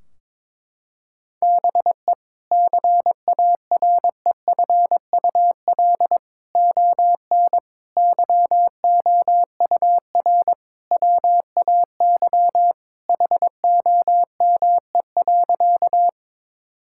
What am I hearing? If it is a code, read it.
Morse code, heard as a keyed tone, about 22 wpm, 700 Hz. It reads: BE CAREFUL ON YOUR WAY HOME.
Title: Sentences from Top 1600 Words 22wpm